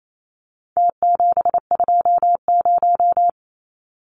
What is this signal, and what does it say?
Morse code, heard as a keyed tone, about 28 wpm, 700 Hz. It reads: T720
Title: Numbers - Contest Style - 4-Digit 28wpm